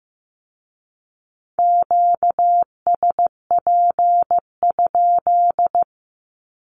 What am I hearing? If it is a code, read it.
Morse code, heard as a keyed tone, about 15 wpm, 700 Hz. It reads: QSP?
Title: Q-Codes with Spoken Meaning - 15wpm